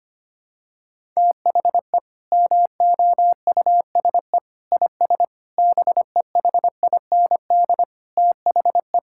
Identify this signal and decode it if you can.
Morse code, heard as a keyed tone, about 25 wpm, 700 Hz. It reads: THE MOUSE IS BEHIND THE
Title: Sentences from Top 1000 Words - Review Part 5 - 25wpm